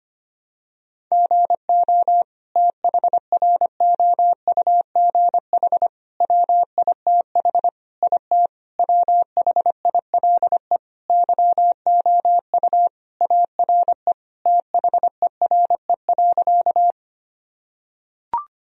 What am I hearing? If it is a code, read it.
Morse code, heard as a keyed tone, about 25 wpm, 700 Hz. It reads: GO THROUGH WITH IT WHILE YOU ARE THERE.
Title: Sentences from Top 1000 Words - Review Part 1 - 25wpm